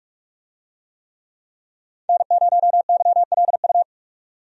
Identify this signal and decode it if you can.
Morse code, heard as a keyed tone, about 46 wpm, 700 Hz. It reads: N0YLU
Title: USA Call Signs - 5 Characters - 45wpm